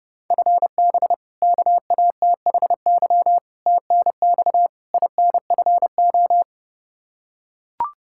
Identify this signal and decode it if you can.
Morse code, heard as a keyed tone, about 30 wpm, 700 Hz. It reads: FB KATHY TNX INFO
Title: Practice QSO - Exchange 3 - More Info - 30wpm